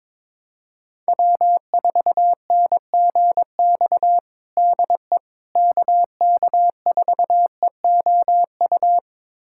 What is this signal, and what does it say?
Morse code, heard as a keyed tone, about 21 wpm, 700 Hz. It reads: W4NGX DE KK4EOU
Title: USA Callsign DE Callsign 22wpm